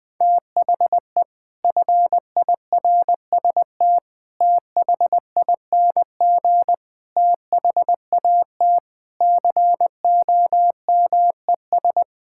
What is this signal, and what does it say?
Morse code, heard as a keyed tone, about 20 wpm, 700 Hz. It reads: THE FIRST THING THAT COMES
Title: Sentences with 8-Word Phrases - 20wpm